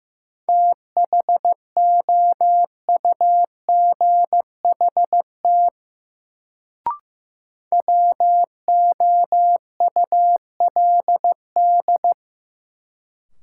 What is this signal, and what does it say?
Morse code, heard as a keyed tone, about 15 wpm, 700 Hz. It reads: THOUGHT WOULD
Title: Sets of 2 Words - New Word Drill - 15wpm